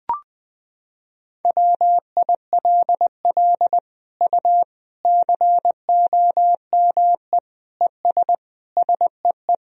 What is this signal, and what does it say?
Morse code, heard as a keyed tone, about 21 wpm, 700 Hz. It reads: WILL U COME ES SEE
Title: Sentences from Top 100 Words and Abbreviations 20wpm